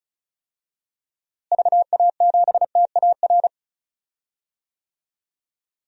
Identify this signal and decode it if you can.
Morse code, heard as a keyed tone, about 36 wpm, 700 Hz. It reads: VA7TAR
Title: DXCC-100 Call Signs - 6 Characters - Repeated 3x - 35wpm